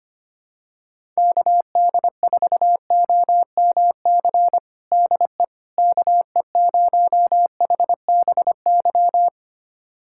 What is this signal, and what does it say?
Morse code, heard as a keyed tone, about 25 wpm, 700 Hz. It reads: KD4OMC DE KE0HBY
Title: USA Callsign DE Callsign - 6 Characters - 25wpm